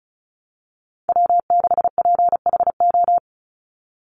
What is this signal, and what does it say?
Morse code, heard as a keyed tone, about 34 wpm, 700 Hz. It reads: W6PHO
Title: Call Signs - 35wpm